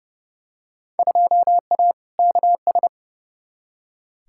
Morse code, heard as a keyed tone, about 31 wpm, 700 Hz. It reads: 2A KS
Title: Field Day Exchanges - 30wpm